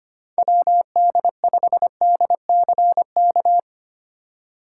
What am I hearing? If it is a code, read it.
Morse code, heard as a keyed tone, about 25 wpm, 700 Hz. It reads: WD5DCK
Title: USA Call Signs - 6 Characters - 3x Character Spacing - 25wpm